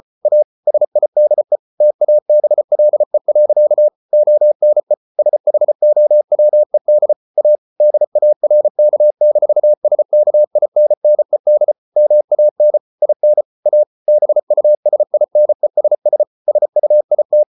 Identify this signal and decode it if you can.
Morse code, heard as a keyed tone, about 34 wpm, 600 Hz. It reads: A SIDE TABLE. ONE SHOWED A DARK-SKINNED MAN IN A BUSINESS SUIT